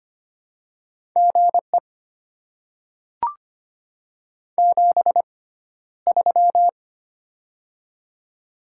Morse code, heard as a keyed tone, about 25 wpm, spaced as for 13 wpm, 700 Hz. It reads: GE 73
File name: Top 20 QSO Elements - 6x Character Spacing - 25wpm